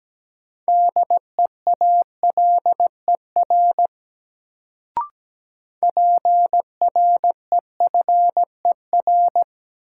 Morse code, heard as a keyed tone, about 17 wpm, 700 Hz. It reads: DEALER PREFER
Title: Sentences from Top 1100 Words - New Word Drill Alt - 17wpm